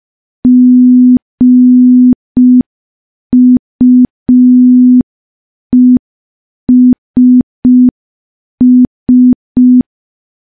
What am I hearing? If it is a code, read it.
Morse code, heard as a keyed tone, about 5 wpm, 250 Hz. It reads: GUESS
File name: morse / morse-training-05WPM